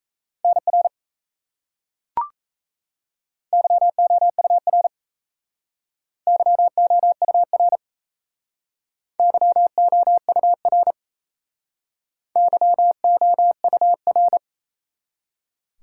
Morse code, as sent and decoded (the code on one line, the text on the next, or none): -. .-. / -.-- --- ..- .-. / -.-- --- ..- .-. / -.-- --- ..- .-. / -.-- --- ..- .-.
NR YOUR YOUR YOUR YOUR